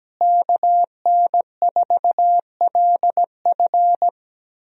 Morse code, heard as a keyed tone, about 17 wpm, 700 Hz. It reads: KN4LF
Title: USA Call Signs - 5 Characters - Repeated 2x - 17wpm